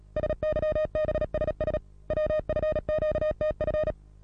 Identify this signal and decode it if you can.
Morse code, heard as a keyed tone, about 37 wpm, 620 Hz. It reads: SYBSS WFQTF
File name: repos / Morsekurs / DJ4UF Morsen27